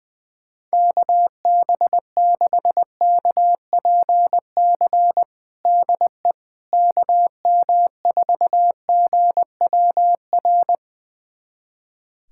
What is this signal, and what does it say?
Morse code, heard as a keyed tone, about 20 wpm, 700 Hz. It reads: KB6KPC DE KM4GWR
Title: USA Callsign DE Callsign - 6 Characters - 20wpm